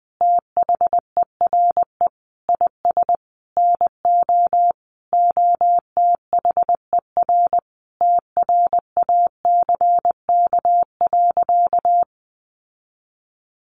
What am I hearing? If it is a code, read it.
Morse code, heard as a keyed tone, about 20 wpm, 700 Hz. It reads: THERE IS NO OTHER TRACK.
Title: Sentences from Top 1100 Words 20wpm